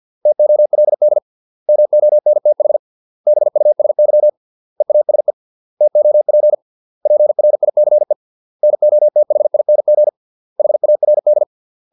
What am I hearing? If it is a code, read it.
Morse code, heard as a keyed tone, about 50 wpm, 600 Hz. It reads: TOLD MONTH BUSY EASE TOP PRICE NOTHING HARD